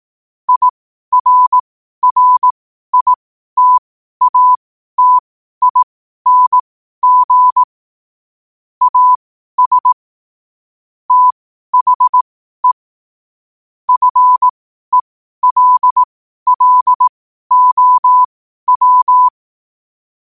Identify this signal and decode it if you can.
Morse code, heard as a keyed tone, about 18 wpm, spaced as for 12 wpm, 1000 Hz. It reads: IRRITATING AS THE FELLOW
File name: qotdfiles / 12wpm